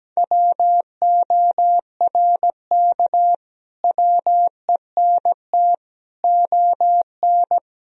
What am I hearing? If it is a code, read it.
Morse code, heard as a keyed tone, about 17 wpm, 700 Hz. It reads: WORK WENT ON
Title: Sentences from Top 1000 Words - Review Part 1 - 17wpm